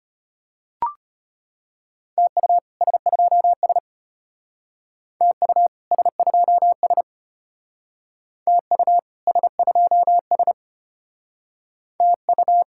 Morse code, heard as a keyed tone, about 33 wpm, 700 Hz. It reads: TU S2S TU S2S TU S2S TU